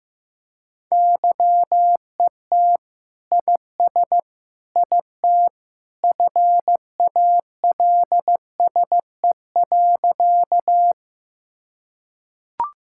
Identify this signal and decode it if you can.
Morse code, heard as a keyed tone, about 15 wpm, 700 Hz. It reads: YET IS IT FALSE.
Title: Sentences from Top 1600 Words 15wpm